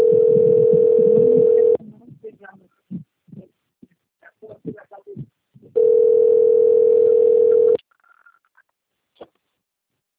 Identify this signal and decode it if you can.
Ringback tone (North America), 440 + 480 Hz.